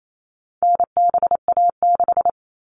Morse code, heard as a keyed tone, about 28 wpm, 700 Hz. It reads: NBA6
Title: All Cut Numbers - 4-Digit 28wpm